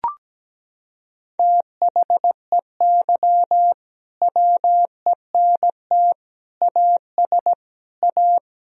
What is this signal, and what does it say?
Morse code, heard as a keyed tone, about 17 wpm, 700 Hz. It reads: THEY WENT AS A
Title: Sentences from Top 1000 Words - Review Part 1 - 17wpm